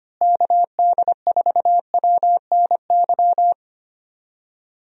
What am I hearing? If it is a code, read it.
Morse code, heard as a keyed tone, about 25 wpm, 700 Hz. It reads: KD4WNY
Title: USA Call Signs - 6 Characters - Repeated 2x - 25wpm